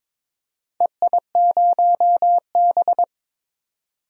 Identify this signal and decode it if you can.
Morse code, heard as a keyed tone, about 22 wpm, 700 Hz. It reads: EI0B
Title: DXCC-100 Call Signs - 4 Characters - 22wpm